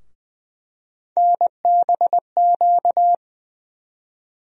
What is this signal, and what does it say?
Morse code, heard as a keyed tone, about 20 wpm, 700 Hz. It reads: NBQ
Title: USA Call Sign Postfixes - 3 Letters 20wpm